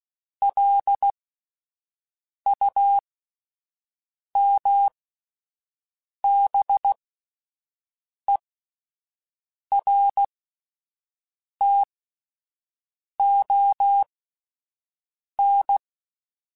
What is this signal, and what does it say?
Morse code, heard as a keyed tone, about 16 wpm, spaced as for 5 wpm, 800 Hz. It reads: LUMBERTON